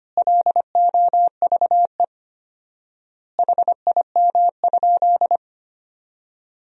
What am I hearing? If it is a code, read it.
Morse code, heard as a keyed tone, about 25 wpm, 700 Hz. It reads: LOVE HIM?
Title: 4x Word-Spacing - Sentences from Top 400 Words - 25wpm